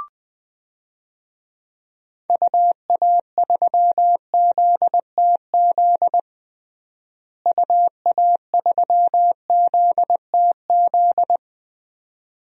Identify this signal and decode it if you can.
Morse code, heard as a keyed tone, about 20 wpm, 700 Hz. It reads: UA3ZTZ UA3ZTZ
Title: DXCC-100 Call Signs - 6 Characters - Repeated 2x - 20wpm